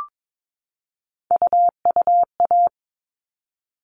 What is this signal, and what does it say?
Morse code, heard as a keyed tone, about 22 wpm, 700 Hz. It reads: UUA
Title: All Cut Numbers - 3-Digit 22wpm